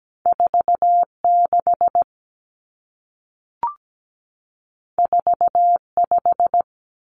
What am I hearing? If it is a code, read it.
Morse code, heard as a keyed tone, about 17 wpm, 700 Hz. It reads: 46 45
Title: Common Cut Numbers - 2-Digit 17wpm